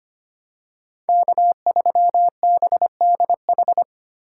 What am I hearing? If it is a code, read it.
Morse code, heard as a keyed tone, about 25 wpm, 700 Hz. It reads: K3BDH
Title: USA Call Signs - 5 Characters - 3x Character Spacing - 25wpm